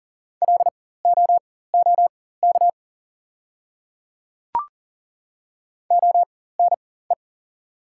Morse code, heard as a keyed tone, about 40 wpm, spaced as for 19 wpm, 690 Hz. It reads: LOOK ONE
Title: 4x Character Spacing - Top 100 Words - 40wpm